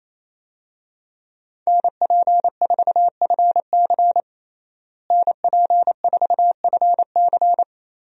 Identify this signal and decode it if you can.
Morse code, heard as a keyed tone, about 28 wpm, 700 Hz. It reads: NP4FC NP4FC
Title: USA Call Signs - 5 Characters - Repeated 3x - 28wpm